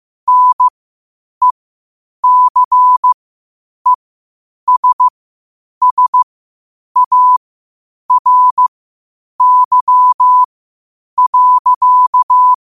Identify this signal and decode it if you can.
Morse code, heard as a keyed tone, about 15 wpm, spaced as for 8 wpm, 1000 Hz. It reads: NECESSARY.
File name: qotdfiles / 8wpm